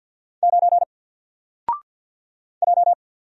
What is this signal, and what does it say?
Morse code, heard as a keyed tone, about 49 wpm, 700 Hz. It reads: 9 J
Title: Single Letter-Number 50wpm